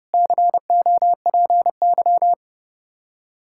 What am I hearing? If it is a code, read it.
Morse code, heard as a keyed tone, about 32 wpm, 700 Hz. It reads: COPY
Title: Practice QSO - Exchange 3 - More Info - 30wpm